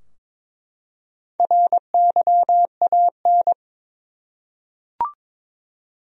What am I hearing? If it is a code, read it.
Morse code, heard as a keyed tone, about 22 wpm, 700 Hz. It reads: RYAN
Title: USA Common Names - Max 4 Letters - 22wpm